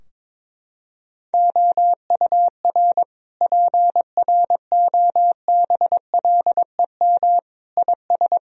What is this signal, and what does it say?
Morse code, heard as a keyed tone, about 22 wpm, 700 Hz. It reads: OUR PROBLEM IS
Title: Sentences from Top 1000 Words - Review Part 2 - 22wpm